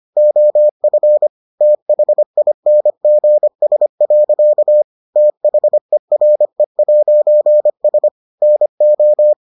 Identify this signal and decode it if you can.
Morse code, heard as a keyed tone, about 25 wpm, 600 Hz. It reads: OF THINGS. THERE'S NO